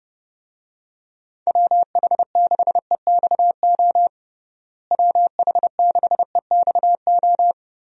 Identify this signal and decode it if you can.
Morse code, heard as a keyed tone, about 32 wpm, 700 Hz. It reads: WH6EXO WH6EXO
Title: USA Call Signs - 6 Characters - Repeated 3x - 30wpm